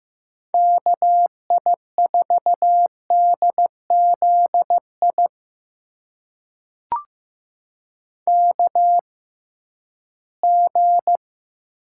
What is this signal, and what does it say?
Morse code, heard as a keyed tone, about 15 wpm, spaced as for 11 wpm, 700 Hz. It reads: KI4DZI KG
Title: USA Call Signs - 6 Characters - 6x Character Spacing - 15wpm